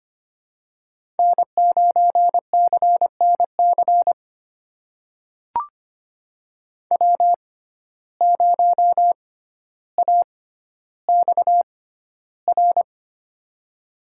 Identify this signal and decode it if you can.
Morse code, heard as a keyed tone, about 25 wpm, spaced as for 13 wpm, 700 Hz. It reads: N9CNC W0AXR
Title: USA Call Signs - 5 Characters - 6x Character Spacing - 25wpm